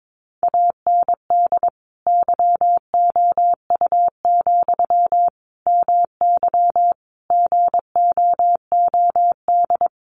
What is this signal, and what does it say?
Morse code, heard as a keyed tone, about 22 wpm, 700 Hz. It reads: AND YOU, MY GOOD
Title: Sentences from Top 1100 Words 22wpm